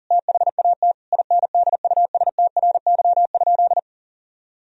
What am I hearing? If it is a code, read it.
Morse code, heard as a keyed tone, about 40 wpm, 700 Hz. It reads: THAT INDUSTRY?
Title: Sentences from Top 1000 Words - Review Part 4 - 40wpm